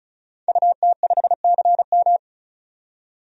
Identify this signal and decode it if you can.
Morse code, heard as a keyed tone, about 36 wpm, 700 Hz. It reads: UT5CM
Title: DXCC-100 Call Signs - 5 Characters - 35wpm